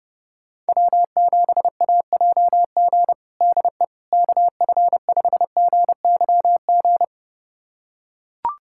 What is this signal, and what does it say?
Morse code, heard as a keyed tone, about 30 wpm, 700 Hz. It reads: W7AJG DE KF5GYG
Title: USA Callsign DE Callsign 30wpm